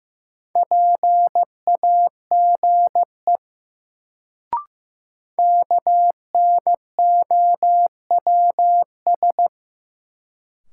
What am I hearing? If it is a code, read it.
Morse code, heard as a keyed tone, about 15 wpm, 700 Hz. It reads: PAGE KNOWS
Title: Sentences from Top 1000 Words - Review Part 1 - New Word Drill - 15wpm